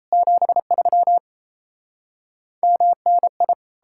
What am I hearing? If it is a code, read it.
Morse code, heard as a keyed tone, about 37 wpm, 700 Hz. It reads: 73 MNI